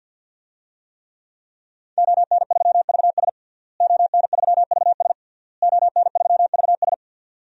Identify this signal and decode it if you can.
Morse code, heard as a keyed tone, about 52 wpm, 700 Hz. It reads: ON3VS ON3VS ON3VS
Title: DXCC-100 Call Signs - 5 Characters - Repeated 3x - 50wpm